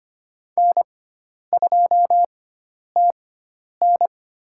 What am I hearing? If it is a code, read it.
Morse code, heard as a keyed tone, about 25 wpm, spaced as for 10 wpm, 700 Hz. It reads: N2TN
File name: Lesson 16g - Letter c - Call Signs 25wpm